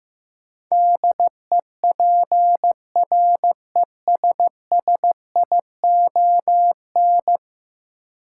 Morse code, heard as a keyed tone, about 15 wpm, 700 Hz. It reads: DEPRESSION
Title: Sentences from Top 1300 Words - New Word Drill Alt - 15wpm